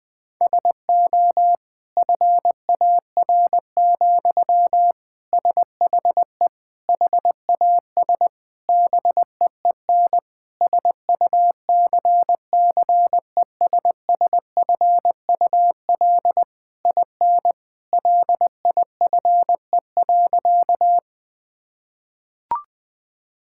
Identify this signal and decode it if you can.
Morse code, heard as a keyed tone, about 21 wpm, 700 Hz. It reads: SO FAR, SHE HAS BEEN SUCCESSFUL IN LIFE.
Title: Sentences from Top 1000 Words - Review Part 6 - 20wpm